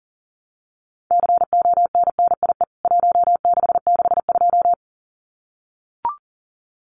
Morse code, heard as a keyed tone, about 40 wpm, 700 Hz. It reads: CONNIE 1662
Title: CWT Exchange 40wpm